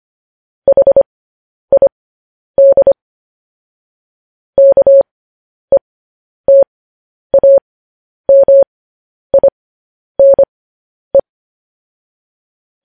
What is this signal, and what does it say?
Morse code, heard as a keyed tone, about 25 wpm, spaced as for 10 wpm, 550 Hz. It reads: HID KETAMINE
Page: News Headlines in Morse Code at 10 WPM